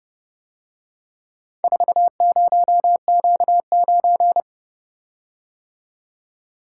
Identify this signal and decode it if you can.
Morse code, heard as a keyed tone, about 29 wpm, 700 Hz. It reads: 40Q9